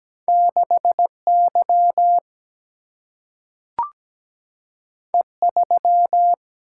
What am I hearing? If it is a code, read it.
Morse code, heard as a keyed tone, about 17 wpm, 700 Hz. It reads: 6Y E3
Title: Single-Letter Single-Number 17wpm